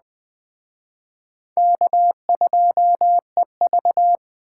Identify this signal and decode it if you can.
Morse code, heard as a keyed tone, about 20 wpm, 700 Hz. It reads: K2EV
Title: Speed Racing - USA 1x2 Call Signs 17wpm